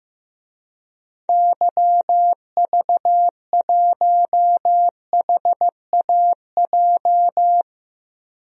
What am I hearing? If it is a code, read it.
Morse code, heard as a keyed tone, about 15 wpm, 700 Hz. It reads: YV1HAJ
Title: DXCC-100 Call Signs - 6 Characters - Repeated 2x - 15wpm